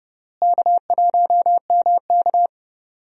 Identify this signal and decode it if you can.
Morse code, heard as a keyed tone, about 30 wpm, 700 Hz. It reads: K1MK
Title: USA Call Signs - 4 Characters - Repeated 3x - 30wpm